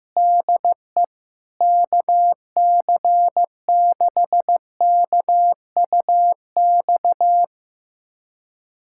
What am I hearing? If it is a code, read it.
Morse code, heard as a keyed tone, about 15 wpm, 700 Hz. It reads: DE KC6KUX
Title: USA Callsign DE Callsign - 6 Characters - 15wpm